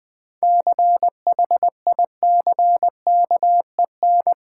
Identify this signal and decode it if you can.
Morse code, heard as a keyed tone, about 20 wpm, 700 Hz. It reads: CHICKEN
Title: Seven Letter Words - 20wpm